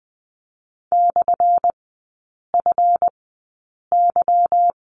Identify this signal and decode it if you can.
Morse code, heard as a keyed tone, about 20 wpm, spaced as for 8 wpm, 700 Hz. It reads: /FY
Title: 178 Lesson 29e - Slash - Three Characters 20wpm